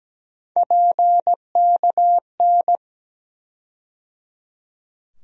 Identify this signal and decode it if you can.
Morse code, heard as a keyed tone, about 17 wpm, 700 Hz. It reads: PKN
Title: USA Call Sign Postfixes - 3 Letters 17wpm